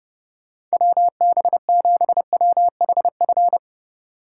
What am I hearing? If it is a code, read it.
Morse code, heard as a keyed tone, about 36 wpm, 700 Hz. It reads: WB7WHF
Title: Speed Racing - USA 2x3 Call Signs 30wpm